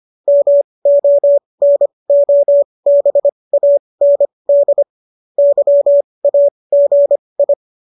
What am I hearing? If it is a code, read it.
Morse code, heard as a keyed tone, about 25 wpm, spaced as for 20 wpm, 570 Hz. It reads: MONOBAND YAGI